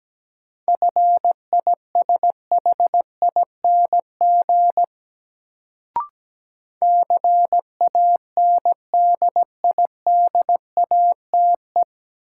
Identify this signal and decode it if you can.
Morse code, heard as a keyed tone, about 17 wpm, 700 Hz. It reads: FISHING CANDIDATE
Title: Sentences from Top 1600 Words - New Word Drill Alt - 17wpm